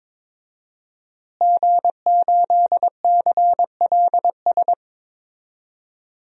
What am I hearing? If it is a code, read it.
Morse code, heard as a keyed tone, about 22 wpm, 700 Hz. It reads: G8CLS
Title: DXCC-100 Call Signs - 5 Characters - Repeated 3x - 22wpm